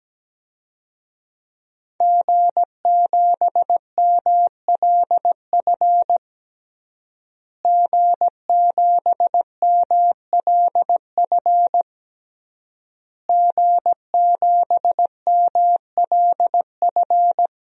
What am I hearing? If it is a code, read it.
Morse code, heard as a keyed tone, about 17 wpm, 700 Hz. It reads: G7MLF G7MLF G7MLF
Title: DXCC-100 Call Signs - 5 Characters - Repeated 3x - 17wpm